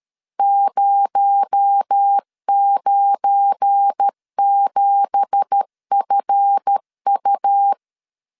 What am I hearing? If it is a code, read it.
Morse code, heard as a keyed tone, about 13 wpm, 800 Hz. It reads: 097FU
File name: CW / DET / 1 9 1500C S A 12wpm 3800